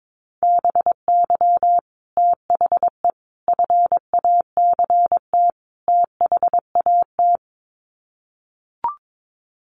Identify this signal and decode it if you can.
Morse code, heard as a keyed tone, about 22 wpm, 700 Hz. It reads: BY THE FACT THAT
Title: Sets of 4 Words 22wpm